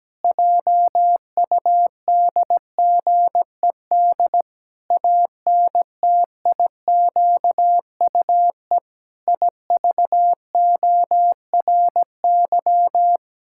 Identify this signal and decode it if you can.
Morse code, heard as a keyed tone, about 17 wpm, 700 Hz. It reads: JUDGED ANTIQUE IVORY
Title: Pangram Challenge v2 - 17wpm